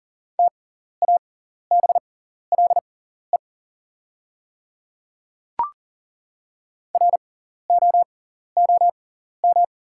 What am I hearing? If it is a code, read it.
Morse code, heard as a keyed tone, about 40 wpm, spaced as for 14 wpm, 690 Hz. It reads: TABLE ROOM
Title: 6x Character Spacing - Top 500 Words - 40wpm